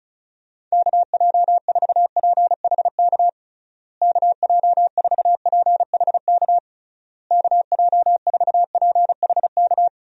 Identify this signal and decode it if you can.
Morse code, heard as a keyed tone, about 34 wpm, 700 Hz. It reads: KJ4PHK KJ4PHK KJ4PHK
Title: USA Call Signs - 6 Characters - Repeated 3x - 35wpm